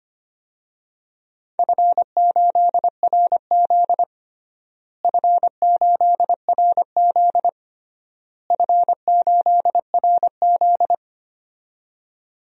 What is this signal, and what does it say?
Morse code, heard as a keyed tone, about 25 wpm, 700 Hz. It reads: F8RZ F8RZ F8RZ
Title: DXCC-100 Call Signs - 4 Characters - Repeated 3x - 25wpm